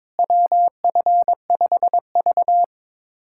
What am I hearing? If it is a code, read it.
Morse code, heard as a keyed tone, about 22 wpm, 700 Hz. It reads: WF5V
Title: USA Callsign DE Callsign - 4 Characters - 22wpm